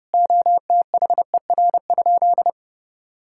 Morse code, heard as a keyed tone, about 32 wpm, 700 Hz. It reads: OTHER?
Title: Sentences from Top 1000 Words - Review Part 6 - 30wpm